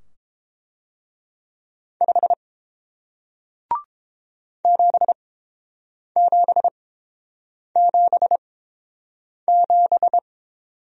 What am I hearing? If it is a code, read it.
Morse code, heard as a keyed tone, about 30 wpm, spaced as for 7 wpm, 700 Hz. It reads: 5 7777